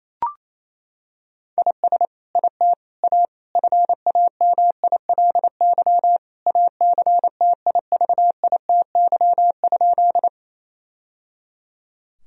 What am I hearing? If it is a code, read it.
Morse code, heard as a keyed tone, about 28 wpm, 700 Hz. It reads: IS IT A FAMILY ACTIVITY?
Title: Sentences from Top 1000 Words - Review Part 4 - 28wpm